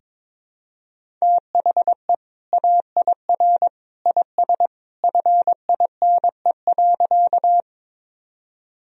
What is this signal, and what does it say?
Morse code, heard as a keyed tone, about 22 wpm, 700 Hz. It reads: THE AIR IS FINE.
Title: Sentences from Top 1000 Words - Review Part 1 - 22wpm